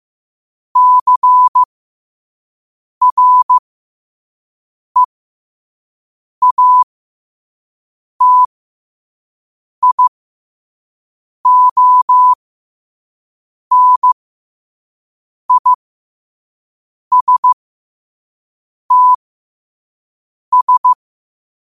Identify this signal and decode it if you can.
Morse code, heard as a keyed tone, about 15 wpm, spaced as for 5 wpm, 1000 Hz. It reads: CREATIONISTS
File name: qotdfiles / 5wpm